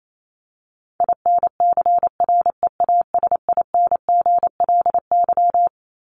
Morse code, heard as a keyed tone, about 28 wpm, 700 Hz. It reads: INCREASINGLY
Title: LY Suffix - Top 100 Words with Base Words 28wpm